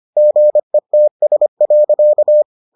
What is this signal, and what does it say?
Morse code, heard as a keyed tone, about 25 wpm, 600 Hz. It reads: GETS.